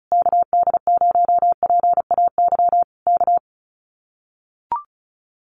Morse code, heard as a keyed tone, about 34 wpm, 700 Hz. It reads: KD0PAY K
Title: Common QSO Phrases Encore 35wpm